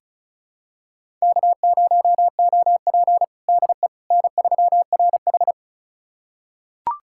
Morse code, heard as a keyed tone, about 36 wpm, 700 Hz. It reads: K0OP DE N3RH
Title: USA Callsign DE Callsign - 4 Characters - 35wpm